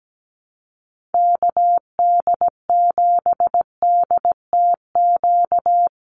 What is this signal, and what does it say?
Morse code, heard as a keyed tone, about 17 wpm, 700 Hz. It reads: KD7DTQ
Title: USA Call Signs 17wpm